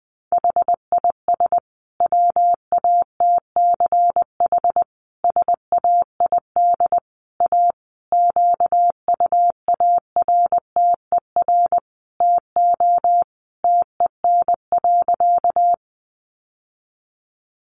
Morse code, heard as a keyed tone, about 20 wpm, 700 Hz. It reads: HIS WATCH SAID A QUARTER TO TEN.
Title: Sentences from Top 1300 Words 20wpm